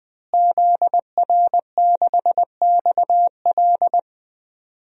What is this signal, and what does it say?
Morse code, heard as a keyed tone, about 20 wpm, 700 Hz. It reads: ZR6XL
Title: DXCC-100 Call Signs - 5 Characters - Repeated 2x - 20wpm